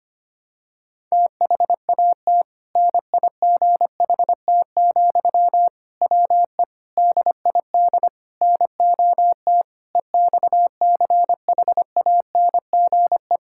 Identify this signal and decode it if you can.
Morse code, heard as a keyed tone, about 25 wpm, 700 Hz. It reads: THAT NIGHT, WE DID NOT EXCHANGE
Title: Sentences from Top 1000 Words - Review Part 6 - 25wpm